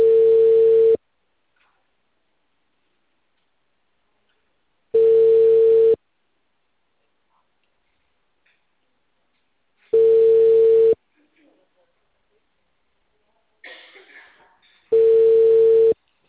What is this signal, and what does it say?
Ringback tone (China, 450 Hz), with speech.
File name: out- -2004- - - .9